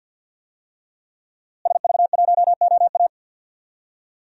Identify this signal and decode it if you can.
Morse code, heard as a keyed tone, about 52 wpm, 700 Hz. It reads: SV1OA
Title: DXCC-100 Call Signs - 5 Characters - Repeated 3x - 50wpm